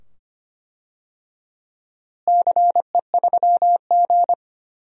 Morse code, heard as a keyed tone, about 25 wpm, 700 Hz. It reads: CE3G
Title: DXCC-100 Call Signs - 4 Characters - Repeated 2x - 25wpm